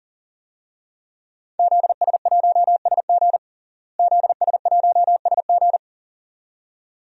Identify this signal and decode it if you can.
Morse code, heard as a keyed tone, about 37 wpm, 700 Hz. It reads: ZS1SG ZS1SG
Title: DXCC-100 Call Signs - 5 Characters - Repeated 2x - 40wpm